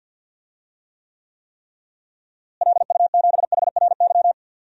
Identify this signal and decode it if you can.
Morse code, heard as a keyed tone, about 53 wpm, 700 Hz. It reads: LU7HRY
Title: DXCC-100 Call Signs - 6 Characters - Repeated 3x - 50wpm